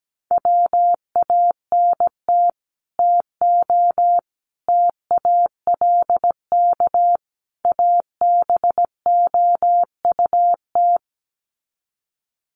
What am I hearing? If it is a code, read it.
Morse code, heard as a keyed tone, about 17 wpm, 700 Hz. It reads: WANT TO TALK ABOUT
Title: Sets of 4 Words 17wpm